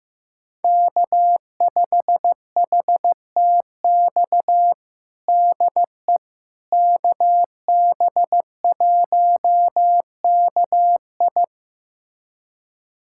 Morse code, heard as a keyed tone, about 15 wpm, 700 Hz. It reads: K5HTX DE KB1KI
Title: USA Callsign DE Callsign - 5 Characters - 15wpm